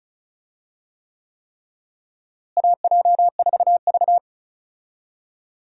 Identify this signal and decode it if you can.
Morse code, heard as a keyed tone, about 34 wpm, 700 Hz. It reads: AJ4V
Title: USA Call Signs - 4 Characters - Repeated 3x - 35wpm